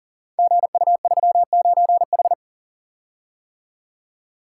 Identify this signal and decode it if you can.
Morse code, heard as a keyed tone, about 41 wpm, 700 Hz. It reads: GU39H
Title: Code Groups - 5 Letter:Numbers - 40wpm